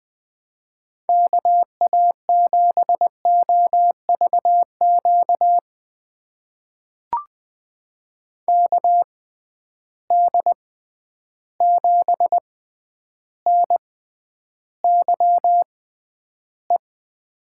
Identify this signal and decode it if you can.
Morse code, heard as a keyed tone, about 20 wpm, spaced as for 10 wpm, 700 Hz. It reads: KA7OVQ KD7NYE
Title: USA Call Signs - 6 Characters - 6x Character Spacing - 20wpm